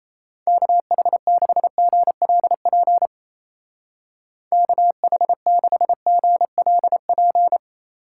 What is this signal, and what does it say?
Morse code, heard as a keyed tone, about 35 wpm, 700 Hz. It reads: KH6GLP KH6GLP
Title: Speed Racing - USA 2x3 Call Signs 28wpm